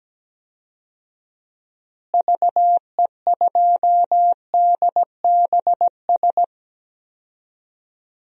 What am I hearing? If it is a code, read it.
Morse code, heard as a keyed tone, about 17 wpm, 700 Hz. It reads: VE2DBS
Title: DXCC-100 Call Signs - 6 Characters - Repeated 2x - 17wpm